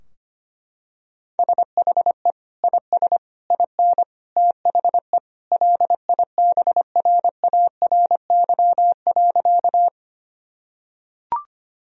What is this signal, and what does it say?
Morse code, heard as a keyed tone, about 25 wpm, 700 Hz. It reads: SHE IS IN THE LIBRARY.
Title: Sentences from Top 1000 Words - Review Part 4 - 25wpm